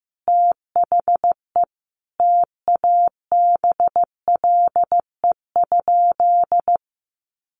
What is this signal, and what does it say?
Morse code, heard as a keyed tone, about 15 wpm, 700 Hz. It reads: THE TABLE?
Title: Sentences from Top 1300 Words 15wpm